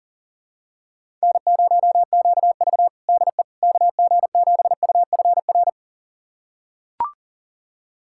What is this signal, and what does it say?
Morse code, heard as a keyed tone, about 40 wpm, 700 Hz. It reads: N0QV DE KG7UFR
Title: USA Callsign DE Callsign 40wpm